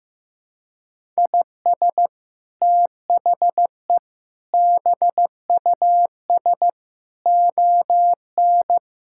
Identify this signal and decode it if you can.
Morse code, heard as a keyed tone, about 15 wpm, 700 Hz. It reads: IS THE BUS ON
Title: Sentences from Top 1000 Words - Review Part 6 - 15wpm